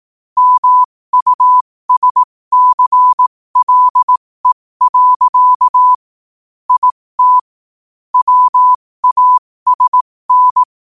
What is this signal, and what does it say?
Morse code, heard as a keyed tone, about 18 wpm, spaced as for 15 wpm, 1000 Hz. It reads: MUSCLE. IT WASN
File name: qotdfiles / 15wpm